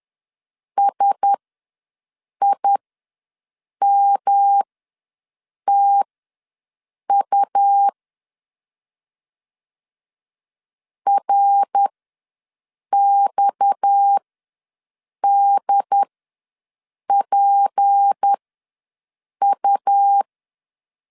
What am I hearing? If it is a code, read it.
Morse code, heard as a keyed tone, about 10 wpm, spaced as for 6 wpm, 800 Hz. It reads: SIMTU RXDPU
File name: LEZIONI / 7 GF9Z / 7 GF ws 10wpm 3918